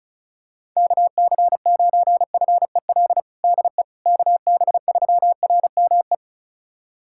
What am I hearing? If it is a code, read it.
Morse code, heard as a keyed tone, about 36 wpm, 700 Hz. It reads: KC9FEL DE KB3RME
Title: USA Callsign DE Callsign - 6 Characters - 35wpm